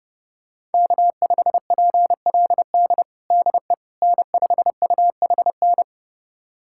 Morse code, heard as a keyed tone, about 32 wpm, 700 Hz. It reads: K5PLD DE N5UHN
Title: USA Callsign DE Callsign - 5 Characters - 30wpm